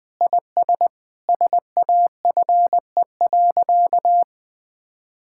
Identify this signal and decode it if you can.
Morse code, heard as a keyed tone, about 20 wpm, 700 Hz. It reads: IS SAFE.
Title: Sentences with 6-Word Phrases - 20wpm